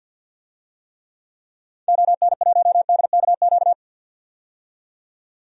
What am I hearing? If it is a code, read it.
Morse code, heard as a keyed tone, about 49 wpm, 700 Hz. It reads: ON1DKQ
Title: DXCC-100 Call Signs - 6 Characters - 50wpm